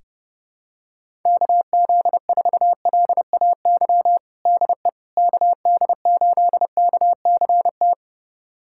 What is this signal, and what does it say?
Morse code, heard as a keyed tone, about 32 wpm, 700 Hz. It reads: KZ4LAY DE KD8KCT
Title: USA Callsign DE Callsign - 6 Characters - 30wpm